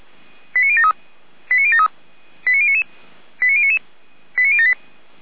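Five-tone selective call (ZVEI), heard as quick runs of five tones, tones of about 70 ms. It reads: 89073 89073 89099 89099 89078
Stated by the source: ZVEI five-tone selective call 89168, received by SDR.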